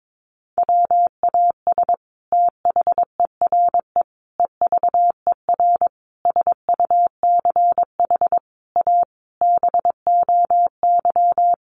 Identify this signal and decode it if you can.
Morse code, heard as a keyed tone, about 22 wpm, 700 Hz. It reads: WAS THERE EVER SUCH A BOY
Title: Sentences from Top 400 Words 22wpm